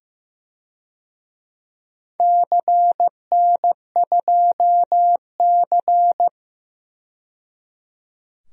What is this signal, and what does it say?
Morse code, heard as a keyed tone, about 15 wpm, 700 Hz. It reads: CN2C
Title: DXCC-100 Call Signs - 4 Characters - 15wpm